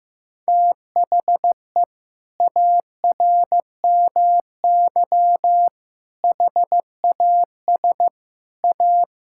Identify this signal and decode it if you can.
Morse code, heard as a keyed tone, about 16 wpm, 700 Hz. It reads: THE ARMY HAS A
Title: Sentences from Top 1000 Words - Review Part 5 - 15wpm